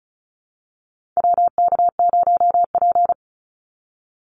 Morse code, heard as a keyed tone, about 34 wpm, 700 Hz. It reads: WK0P
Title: Call Signs - 35wpm